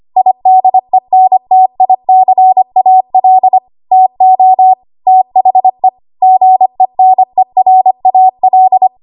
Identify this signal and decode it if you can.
Morse code, heard as a keyed tone, about 25 wpm, 750 Hz. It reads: IDENTICAL TO THE GENERAL
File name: Archive / 25 WPM / 25WPM